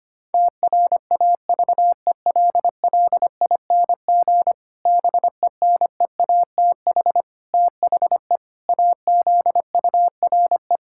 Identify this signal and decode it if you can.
Morse code, heard as a keyed tone, about 25 wpm, 700 Hz. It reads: TRAVELLING BENEATH THE AZURE
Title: Pangram Challenge v2 - 25wpm